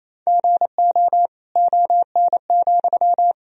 Morse code, heard as a keyed tone, about 28 wpm, 700 Hz. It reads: GO ON,
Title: Sentences from Top 1000 Words - Review Part 4 - 28wpm